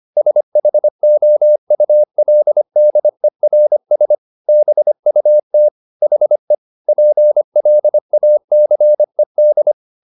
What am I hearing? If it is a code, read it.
Morse code, heard as a keyed tone, about 25 wpm, 600 Hz. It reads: SHOULDERS BUT HE PLACED